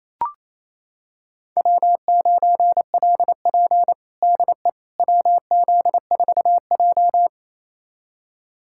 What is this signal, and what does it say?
Morse code, heard as a keyed tone, about 28 wpm, 690 Hz. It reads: W9LP DE WZ4J
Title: USA Callsign DE Callsign - 4 Characters - 28wpm